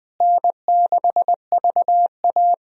Morse code, heard as a keyed tone, about 20 wpm, 700 Hz. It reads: N6VA
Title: USA Call Signs - 4 Characters - Repeated 3x - 20wpm